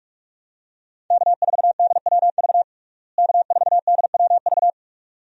Morse code, heard as a keyed tone, about 47 wpm, 700 Hz. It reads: K4DWV K4DWV
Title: USA Call Signs - 5 Characters - Repeated 2x - 45wpm